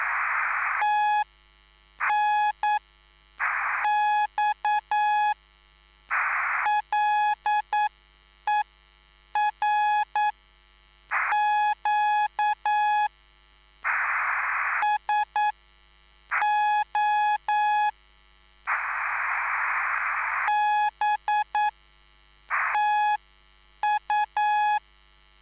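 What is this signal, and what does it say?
Morse code, heard as a keyed tone, about 9 wpm, spaced as for 6 wpm, 840 Hz. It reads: TNX LER QSO B TU
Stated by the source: Live CW QSO between PA2BX and SP3JFK on the 14055 kHz QRS frequency, at a basic 8 wpm.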